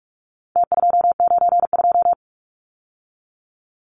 Morse code, heard as a keyed tone, about 44 wpm, 700 Hz. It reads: T292
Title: Numbers - Contest Style - 4-Digit 45wpm